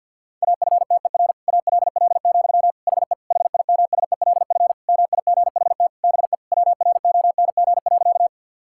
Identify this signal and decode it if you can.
Morse code, heard as a keyed tone, about 50 wpm, 700 Hz. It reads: AFTER ALL, HE HIMSELF MIGHT BE WRONG.
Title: Sentences from Top 1600 Words 50wpm